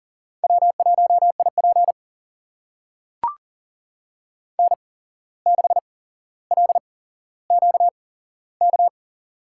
Morse code, heard as a keyed tone, about 40 wpm, spaced as for 16 wpm, 700 Hz. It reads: W1IP N6LQK
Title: USA Call Signs - 8x Character Spacing - 40wpm